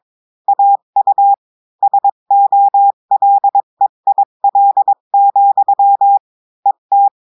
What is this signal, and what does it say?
Morse code, heard as a keyed tone, about 22 wpm, 800 Hz. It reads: AU SOLEIL, ET
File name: cours-du-mercredi / audios / cours-853-du-2-mai-2024-etxte chiffre et texte clair-a-20-wpm